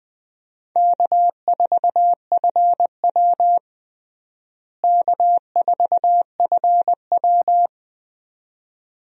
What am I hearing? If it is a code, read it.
Morse code, heard as a keyed tone, about 21 wpm, 700 Hz. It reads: K4FW K4FW
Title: USA Call Signs - 4 Characters - Repeated 2x - 20wpm